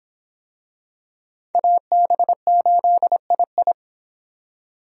Morse code, heard as a keyed tone, about 26 wpm, 700 Hz. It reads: AB8II